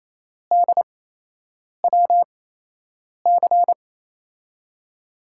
Morse code, heard as a keyed tone, about 28 wpm, spaced as for 8 wpm, 700 Hz. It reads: DWC
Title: USA Call Signs - 5 Characters - 8x Character Spacing - 28wpm